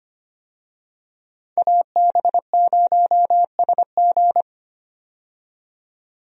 Morse code, heard as a keyed tone, about 25 wpm, 700 Hz. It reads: AB0SG
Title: USA Call Signs - 5 Characters - 25wpm